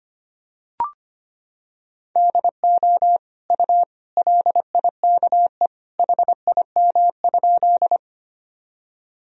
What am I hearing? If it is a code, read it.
Morse code, heard as a keyed tone, about 25 wpm, 700 Hz. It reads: DO U LIKE HIM?
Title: Sentences from Top 100 Words and Abbreviations 25wpm